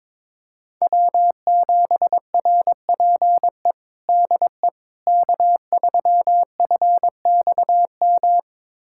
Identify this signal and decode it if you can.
Morse code, heard as a keyed tone, about 22 wpm, 700 Hz. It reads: W7RPE DE K3FXM
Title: USA Callsign DE Callsign - 5 Characters - 22wpm